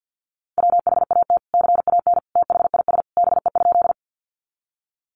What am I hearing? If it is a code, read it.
Morse code, heard as a keyed tone, about 50 wpm, 700 Hz. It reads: WHAT CAN THIS BE?
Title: Sentences from Top 100 Words 50wpm